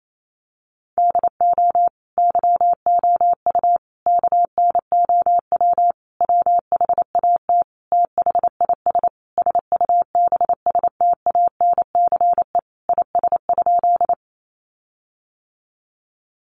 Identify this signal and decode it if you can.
Morse code, heard as a keyed tone, about 28 wpm, 700 Hz. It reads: DO YOU KNOW WHAT THIS SUBSTANCE IS?
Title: Sentences from Top 1500 Words 28wpm